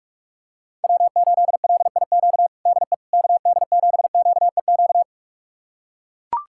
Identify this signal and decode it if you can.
Morse code, heard as a keyed tone, about 41 wpm, 700 Hz. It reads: W8LIQ DE KD7QEQ
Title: USA Callsign DE Callsign 45wpm